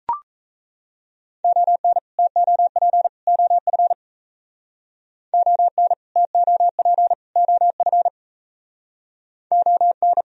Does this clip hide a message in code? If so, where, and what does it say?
Yes: Morse code from 1.4 to 10.2 s, heard as a keyed tone, about 37 wpm, 700 Hz. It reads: ON TOP OF ON TOP OF ON